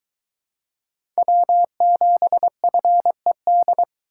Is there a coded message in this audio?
Yes: Morse code from 1.2 to 3.8 s, heard as a keyed tone, about 21 wpm, 700 Hz. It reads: W7FED